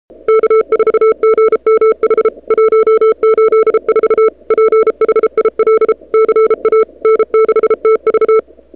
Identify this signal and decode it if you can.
Morse code, heard as a keyed tone, about 33 wpm, 440 Hz. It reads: K4GMH 184 PHIL CA N6TV